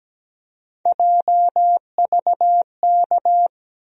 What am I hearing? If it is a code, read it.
Morse code, heard as a keyed tone, about 17 wpm, 700 Hz. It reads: JVK
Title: USA Call Sign Postfixes - 3 Letters 17wpm